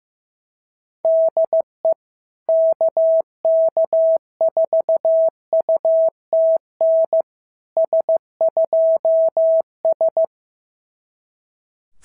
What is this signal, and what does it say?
Morse code, heard as a keyed tone, about 15 wpm, 650 Hz. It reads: DE KK4UTN S2S
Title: SOTA QSO Practice 15wpm